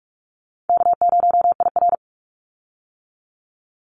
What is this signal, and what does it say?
Morse code, heard as a keyed tone, about 44 wpm, 700 Hz. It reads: K0IR
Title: Easy Call Signs - 45wpm